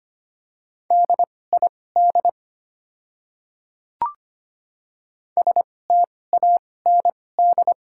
Morse code, heard as a keyed tone, about 25 wpm, spaced as for 18 wpm, 700 Hz. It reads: DID STAND
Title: 2x Character Spacing - Top 300 Words - 25wpm